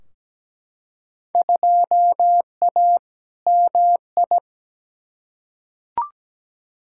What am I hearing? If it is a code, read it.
Morse code, heard as a keyed tone, about 17 wpm, 700 Hz. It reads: 2A MI
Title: Field Day Exchanges Encore - 17wpm